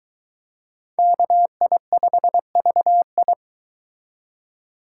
Morse code, heard as a keyed tone, about 22 wpm, 700 Hz. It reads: KI5VI